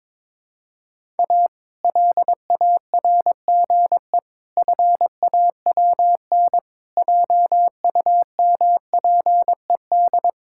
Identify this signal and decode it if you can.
Morse code, heard as a keyed tone, about 22 wpm, 700 Hz. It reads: A LARGE FAWN JUMPED
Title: Pangram Challenge v2 - 22wpm